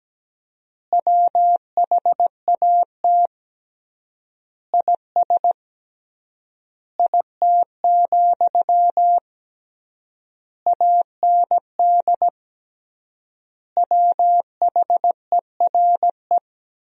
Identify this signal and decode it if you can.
Morse code, heard as a keyed tone, about 17 wpm, 700 Hz. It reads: WHAT IS IT, AND WHERE
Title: 3x Word-Spacing - Sentences from Top 300 Words - 17wpm